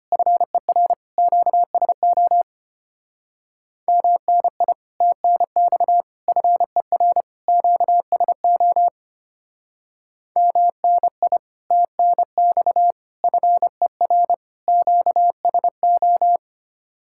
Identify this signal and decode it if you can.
Morse code, heard as a keyed tone, about 32 wpm, 700 Hz. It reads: FER QSO MNI TNX FER QSO MNI TNX FER QSO